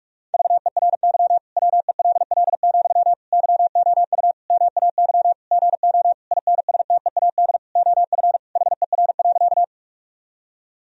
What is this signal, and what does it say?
Morse code, heard as a keyed tone, about 41 wpm, 700 Hz. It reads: VERY WELL, YOU MAY GO INSTEAD OF HER.
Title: Sentences from Top 1000 Words - Review Part 4 - 45wpm